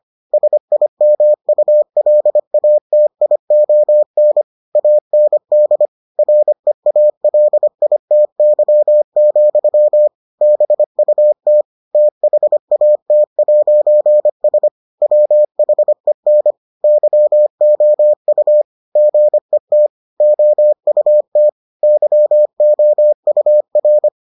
Morse code, heard as a keyed tone, about 25 wpm, 600 Hz. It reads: SIMULATION AND REALITY, BUT THAT'S WHEN YOU GET OUT YOUR